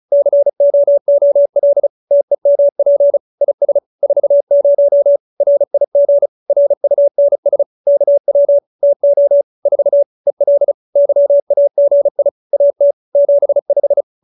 Morse code, heard as a keyed tone, about 34 wpm, 570 Hz. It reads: COOL TEMP IS 40 RIG RUNS KW TO 4 EL YAGI AT 75
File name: QSO302 35